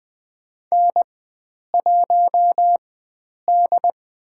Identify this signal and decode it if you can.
Morse code, heard as a keyed tone, about 20 wpm, spaced as for 9 wpm, 700 Hz. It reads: N1D
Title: USA Call Signs - 4x Character Spacing - 20wpm